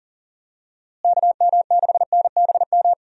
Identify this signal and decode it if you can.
Morse code, heard as a keyed tone, about 40 wpm, 700 Hz. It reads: KM6NBM
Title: USA Call Signs - 6 Characters - Repeated 2x - 40wpm